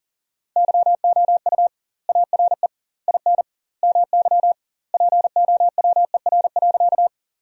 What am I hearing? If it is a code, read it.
Morse code, heard as a keyed tone, about 40 wpm, 700 Hz. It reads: YOU ARE IN MY POWER.
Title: 2x Word-Spacing - Sentences from Top 500 Words - 40wpm